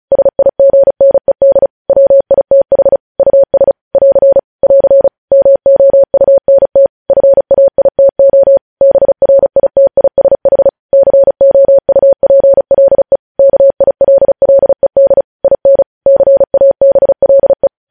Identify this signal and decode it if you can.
Morse code, heard as a keyed tone, about 36 wpm, 550 Hz. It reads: SIGNED WITH US <AR> <AR> MOUNT FAITO BRITISH COUPLE KILLED IN CABLE
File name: podcast / News headlines-1x-35x35